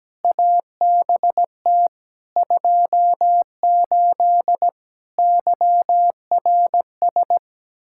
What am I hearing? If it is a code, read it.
Morse code, heard as a keyed tone, about 17 wpm, 700 Hz. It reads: ABT 28 YRS
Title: Practice QSO - Exchange 3 - More Info - 17wpm